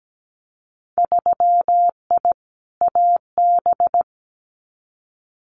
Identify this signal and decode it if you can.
Morse code, heard as a keyed tone, about 17 wpm, 700 Hz. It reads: 3I AB
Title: Winter Field Day Exchanges Encore 17wpm